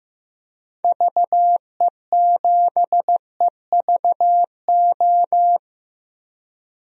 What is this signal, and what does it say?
Morse code, heard as a keyed tone, about 15 wpm, 700 Hz. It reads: VE7EVO
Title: DXCC-100 Call Signs - 6 Characters - Repeated 2x - 15wpm